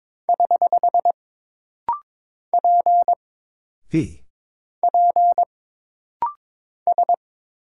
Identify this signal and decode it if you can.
Morse code, heard as a keyed tone, about 22 wpm, 700 Hz. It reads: <HH> P P S
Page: This rapid-fire timing is geared to push your ICR (Instant Character Recognition) skills.